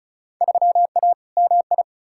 Morse code, heard as a keyed tone, about 34 wpm, 700 Hz. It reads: 3A MI
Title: Field Day Exchanges - 35wpm